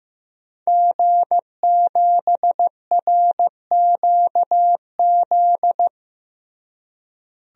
Morse code, heard as a keyed tone, about 16 wpm, 700 Hz. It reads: G7RQZ
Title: DXCC-100 Call Signs - 5 Characters - Repeated 2x - 15wpm